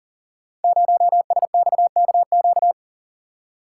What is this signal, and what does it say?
Morse code, heard as a keyed tone, about 39 wpm, 700 Hz. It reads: 0SXKQ
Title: Code Groups - 5 Letter:Numbers - 40wpm